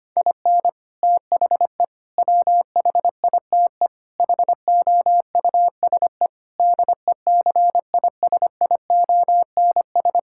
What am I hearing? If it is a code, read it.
Morse code, heard as a keyed tone, about 25 wpm, 700 Hz. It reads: IN THE WHITE HOUSE DECISIONS
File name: practice / 25wpm / Sentences with 4-Word Phrases - 25wpm